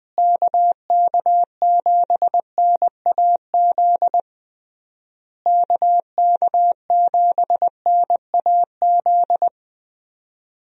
Morse code, heard as a keyed tone, about 20 wpm, 700 Hz. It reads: KK7NAZ KK7NAZ
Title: USA Call Signs - 6 Characters - Repeated 3x - 20wpm